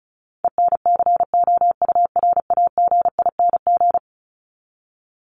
Morse code, heard as a keyed tone, about 34 wpm, 700 Hz. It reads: ENCOURAGING
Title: Top 1751-2000 Words 35wpm